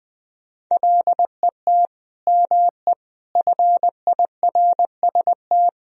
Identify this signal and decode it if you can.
Morse code, heard as a keyed tone, about 20 wpm, 700 Hz. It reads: LET ME FIRST
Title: Sentences from Top 1000 Words - Review Part 2 - 20wpm